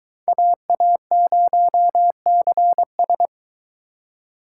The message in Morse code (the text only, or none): AA0CS